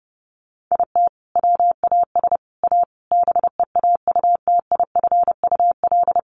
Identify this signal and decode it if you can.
Morse code, heard as a keyed tone, about 30 wpm, 700 Hz. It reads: IT WAS A BEAUTIFUL
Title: Sentences from Top 1100 Words 30wpm